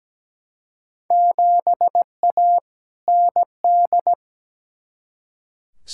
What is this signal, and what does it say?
Morse code, heard as a keyed tone, about 17 wpm, 700 Hz. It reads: 7A ND
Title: Field Day Exchanges Encore - 17wpm